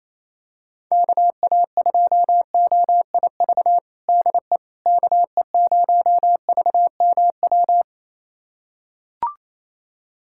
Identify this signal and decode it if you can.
Morse code, heard as a keyed tone, about 28 wpm, 700 Hz. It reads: KA2OIV DE KE0VMW
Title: USA Callsign DE Callsign - 6 Characters - 28wpm